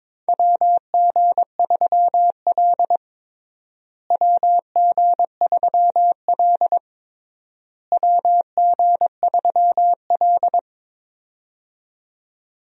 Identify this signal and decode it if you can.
Morse code, heard as a keyed tone, about 22 wpm, 700 Hz. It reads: WG3L WG3L WG3L
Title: USA Call Signs - 4 Characters - Repeated 3x - 22wpm